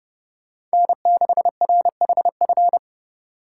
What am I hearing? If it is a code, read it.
Morse code, heard as a keyed tone, about 32 wpm, 700 Hz. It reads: N6RHF